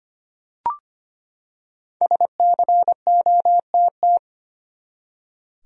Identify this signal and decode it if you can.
Morse code, heard as a keyed tone, about 25 wpm, 700 Hz. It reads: SCOTT
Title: USA Common Names - Max 5 Letters - 25wpm